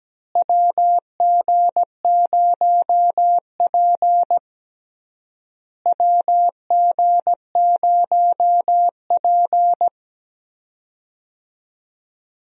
Morse code, heard as a keyed tone, about 17 wpm, 700 Hz. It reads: WG0P WG0P
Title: USA Call Signs - 4 Characters - Repeated 3x - 17wpm